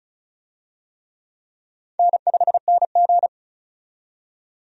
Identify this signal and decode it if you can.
Morse code, heard as a keyed tone, about 36 wpm, 700 Hz. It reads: N5NG
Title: USA Call Signs - 4 Characters - 35wpm